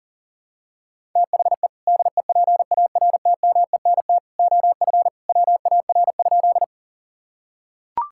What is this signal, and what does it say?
Morse code, heard as a keyed tone, about 40 wpm, 700 Hz. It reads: THE DEPARTMENT OF WAR?
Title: Sentences from Top 1000 Words - Review Part 5 - 40wpm